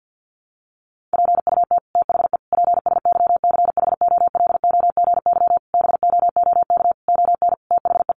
Morse code, heard as a keyed tone, about 49 wpm, 700 Hz. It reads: PUT THE PSYCHOLOGY BOOK ON THE
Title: Sentences from Top 1400 Words 50wpm